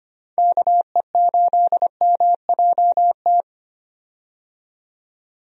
Morse code, heard as a keyed tone, about 25 wpm, 700 Hz. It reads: KE8MJT
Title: USA Call Signs - 6 Characters - 25wpm